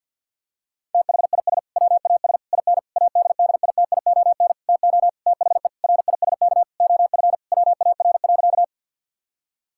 Morse code, heard as a keyed tone, about 50 wpm, 700 Hz. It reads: THIS WAS IN ADDITION TO THE RISK OF WAR.
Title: Sentences from Top 1000 Words - Review Part 6 - 50wpm